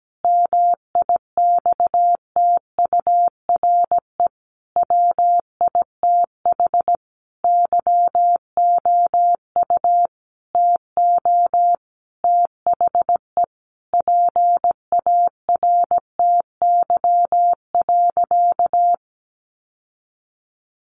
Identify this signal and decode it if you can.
Morse code, heard as a keyed tone, about 17 wpm, 700 Hz. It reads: MIXTURE WITH YOU TO THE PARTY.
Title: Sentences from Top 1500 Words 17wpm